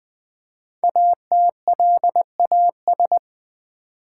Morse code, heard as a keyed tone, about 20 wpm, 700 Hz. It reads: ATLAS
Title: Easy Words with Hints - 20wpm